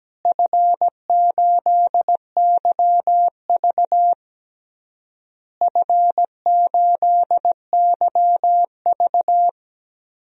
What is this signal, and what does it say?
Morse code, heard as a keyed tone, about 17 wpm, 700 Hz. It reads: F8YV F8YV
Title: DXCC-100 Call Signs - 4 Characters - Repeated 3x - 17wpm